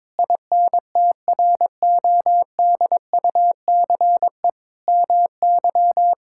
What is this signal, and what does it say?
Morse code, heard as a keyed tone, about 21 wpm, 700 Hz. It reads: INTRODUCE MY
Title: Sentences from Top 1600 Words 22wpm